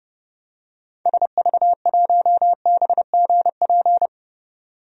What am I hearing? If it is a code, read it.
Morse code, heard as a keyed tone, about 29 wpm, 700 Hz. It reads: SV1BGP
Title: DXCC-100 Call Signs - 6 Characters - 30wpm